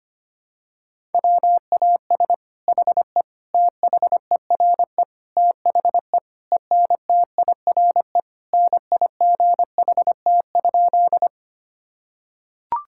Morse code, heard as a keyed tone, about 25 wpm, 700 Hz. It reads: WAS HE THERE THE ENTIRE NIGHT?
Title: Sentences from Top 1000 Words - Review Part 5 - 25wpm